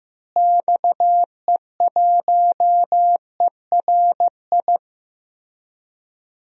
Morse code, heard as a keyed tone, about 16 wpm, 700 Hz. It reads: XE1ERI
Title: DXCC-100 Call Signs - 6 Characters - Repeated 3x - 15wpm